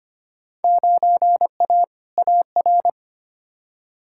Morse code, heard as a keyed tone, about 25 wpm, 700 Hz. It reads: 9A AR
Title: Field Day Exchanges Encore - 25wpm